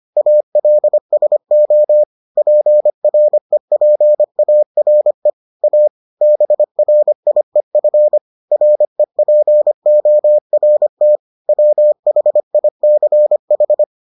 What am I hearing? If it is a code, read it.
Morse code, heard as a keyed tone, about 25 wpm, 600 Hz. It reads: ALSO PREPARE A BRIEF REPORT WHICH